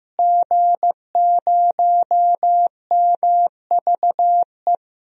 Morse code, heard as a keyed tone, about 15 wpm, 700 Hz. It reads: G0MVE
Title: DXCC-100 Call Signs - 5 Characters - Repeated 3x - 15wpm